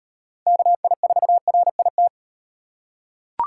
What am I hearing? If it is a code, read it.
Morse code, heard as a keyed tone, about 38 wpm, 700 Hz. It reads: KI4RIT